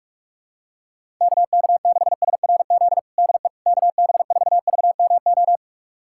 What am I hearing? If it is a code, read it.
Morse code, heard as a keyed tone, about 46 wpm, 700 Hz. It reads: KK6SRZ DE KB4VMO
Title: USA Callsign DE Callsign - 6 Characters - 45wpm